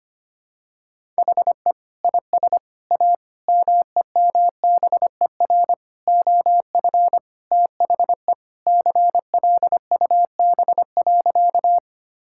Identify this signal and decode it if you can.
Morse code, heard as a keyed tone, about 25 wpm, 700 Hz. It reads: HE IS A MEMBER OF THE CLUB.
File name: practice / 25wpm / Sentences with 3-Word Phrases - 25wpm